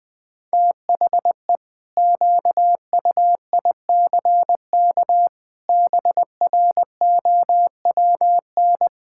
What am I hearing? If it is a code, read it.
Morse code, heard as a keyed tone, about 20 wpm, 700 Hz. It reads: THE QUICK BROWN
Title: Pangram Challenge v2 - 20wpm